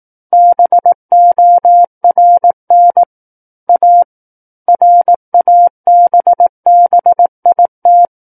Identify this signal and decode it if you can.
Morse code, heard as a keyed tone, about 18 wpm, 700 Hz. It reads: BORN A RABBIT
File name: 18 WPM